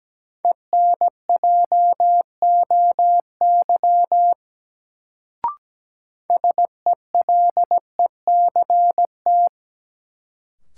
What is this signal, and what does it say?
Morse code, heard as a keyed tone, about 17 wpm, 700 Hz. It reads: ENJOY SELECT
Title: Sentences from Top 1300 Words - New Word Drill Alt - 17wpm